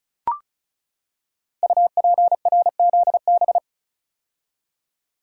Morse code, heard as a keyed tone, about 36 wpm, 700 Hz. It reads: UPRZB
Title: Code Groups - 5 Letters - 35wpm